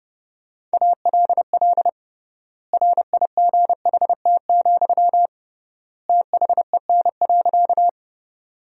Morse code, heard as a keyed tone, about 30 wpm, 700 Hz. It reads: ALL RIGHT, THEN.
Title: 3x Word-Spacing - Sentences from Top 200 Words - 30wpm